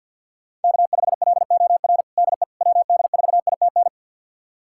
Morse code, heard as a keyed tone, about 53 wpm, 700 Hz. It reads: K5LOR DE WD4ITN
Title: USA Callsign DE Callsign 50wpm